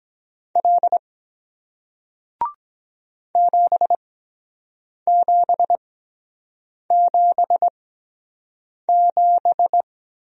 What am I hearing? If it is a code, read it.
Morse code, heard as a keyed tone, about 23 wpm, spaced as for 7 wpm, 700 Hz. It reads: L 7777